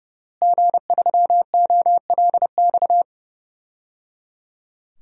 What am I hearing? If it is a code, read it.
Morse code, heard as a keyed tone, about 30 wpm, 700 Hz. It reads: G3OLX
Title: DXCC-100 Call Signs - 5 Characters - Repeated 3x - 30wpm